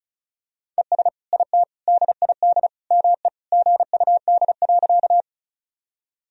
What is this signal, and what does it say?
Morse code, heard as a keyed tone, about 34 wpm, 700 Hz. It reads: ES IT DID ME GUD.
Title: Sentences from Top 200 Words and Abbreviations 35wpm